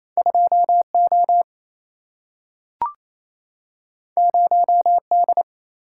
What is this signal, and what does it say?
Morse code, heard as a keyed tone, about 28 wpm, 700 Hz. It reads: 2 O 0 D
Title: Single-Letter Single-Number 28wpm